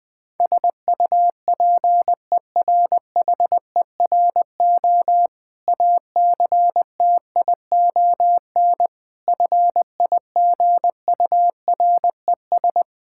Morse code, heard as a keyed tone, about 20 wpm, 690 Hz. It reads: SUPERHERO ACTION FIGURES
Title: Difficult Words with Hints - 20wpm